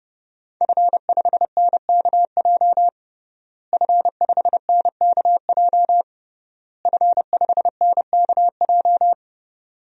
Morse code, heard as a keyed tone, about 30 wpm, 700 Hz. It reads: F5NKJ F5NKJ F5NKJ
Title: DXCC-100 Call Signs - 5 Characters - Repeated 3x - 30wpm